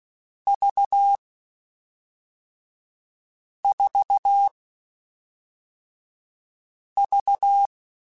Morse code, heard as a keyed tone, about 16 wpm, 800 Hz. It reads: V 4 V
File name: Groups@5 / Groups@5 mp3s / VB46@5